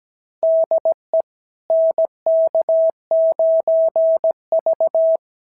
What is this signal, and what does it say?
Morse code, heard as a keyed tone, about 17 wpm, 660 Hz. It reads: DE NK9V
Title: SOTA QSO Practice 17wpm